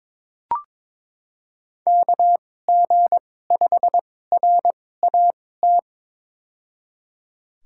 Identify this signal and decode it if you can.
Morse code, heard as a keyed tone, about 22 wpm, spaced as for 16 wpm, 700 Hz. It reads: KG5RAT
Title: USA Call Signs - 6 Characters - 2x Character Spacing - 22wpm